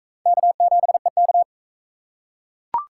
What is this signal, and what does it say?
Morse code, heard as a keyed tone, about 36 wpm, 700 Hz. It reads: K7EK